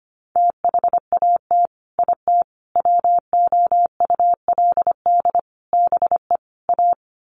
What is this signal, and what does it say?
Morse code, heard as a keyed tone, about 25 wpm, 700 Hz. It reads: THAT IT WOULD BE A
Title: Sets of 5 Words Encore2 25wpm